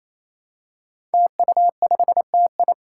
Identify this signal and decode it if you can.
Morse code, heard as a keyed tone, about 28 wpm, 700 Hz. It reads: TU5TI
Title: Five Challenging Letter-Number Combinations 28wpm